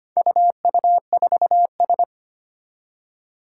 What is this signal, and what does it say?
Morse code, heard as a keyed tone, about 25 wpm, 700 Hz. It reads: UU4S
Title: Four Challenging Letter-Number Combinations 25wpm